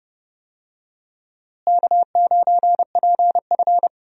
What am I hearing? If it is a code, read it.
Morse code, heard as a keyed tone, about 32 wpm, 700 Hz. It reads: K9PF
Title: USA Call Signs - 4 Characters - Repeated 3x - 30wpm